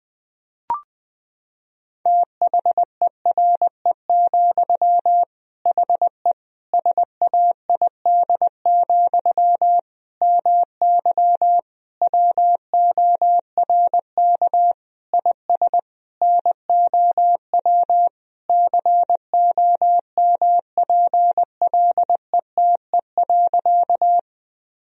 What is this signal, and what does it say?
Morse code, heard as a keyed tone, about 21 wpm, 700 Hz. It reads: THERE, HE SAID, MY WORK IS NOW COMPLETE.
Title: Sentences from Top 1000 Words - Review Part 2 - 20wpm